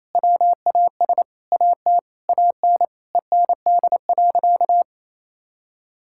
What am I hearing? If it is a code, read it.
Morse code, heard as a keyed tone, about 28 wpm, 700 Hz. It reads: WAS AT AN END.
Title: Sentences from Top 1000 Words - Review Part 6 - 28wpm